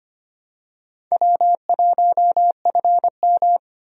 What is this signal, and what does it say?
Morse code, heard as a keyed tone, about 25 wpm, 700 Hz. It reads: W1FM
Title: USA Call Signs - 4 Characters - 25wpm